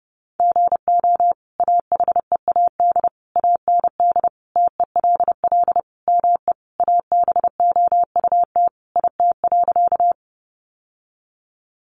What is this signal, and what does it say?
Morse code, heard as a keyed tone, about 30 wpm, 700 Hz. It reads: GO AHEAD AND TELL ME ABOUT IT.
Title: Sentences from Top 1100 Words 30wpm